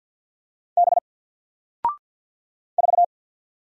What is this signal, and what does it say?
Morse code, heard as a keyed tone, about 50 wpm, 700 Hz. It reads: B 4
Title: Single Letter-Number 50wpm